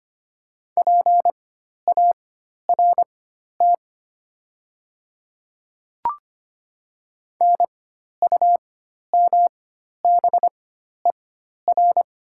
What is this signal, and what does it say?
Morse code, heard as a keyed tone, about 25 wpm, spaced as for 12 wpm, 700 Hz. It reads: PART NUMBER
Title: 4x Character Spacing - Top 200 Words - 25wpm